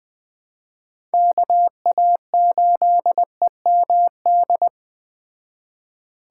Morse code, heard as a keyed tone, about 20 wpm, 700 Hz. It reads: KA8EMD
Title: Speed Racing - USA 2x3 Call Signs 17wpm